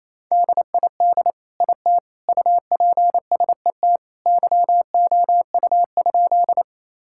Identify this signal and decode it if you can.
Morse code, heard as a keyed tone, about 28 wpm, 700 Hz. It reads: DID IT UPSET YOU?
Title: Sentences from Top 1000 Words - Review Part 5 - 28wpm